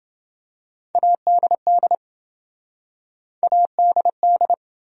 Morse code, heard as a keyed tone, about 27 wpm, 700 Hz. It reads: ADD ADD